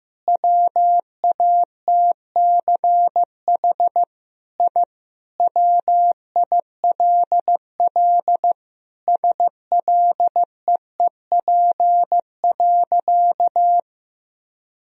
Morse code, heard as a keyed tone, about 15 wpm, 700 Hz. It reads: WATCH I WILL SLEEP.
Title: Sentences from Top 1000 Words - Review Part 2 - 15wpm